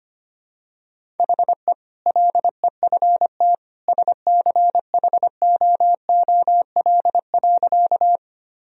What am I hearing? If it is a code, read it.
Morse code, heard as a keyed tone, about 25 wpm, 700 Hz. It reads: HE LEFT SCHOOL.
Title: Sentences from Top 1000 Words - Review Part 1 - 25wpm